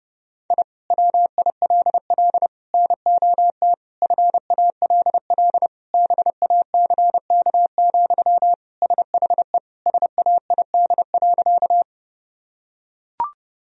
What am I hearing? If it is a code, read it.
Morse code, heard as a keyed tone, about 30 wpm, 700 Hz. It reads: I WILL NOT FALL BACK, SHE SAID.
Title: Sentences from Top 1000 Words - Review Part 2 - 30wpm